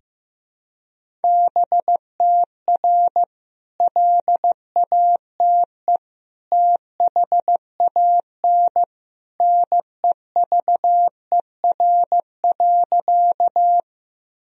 Morse code, heard as a keyed tone, about 15 wpm, 700 Hz. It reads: BTR LATE THAN NEVER.
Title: Sentences from Top 400 Words and Abbreviations 15wpm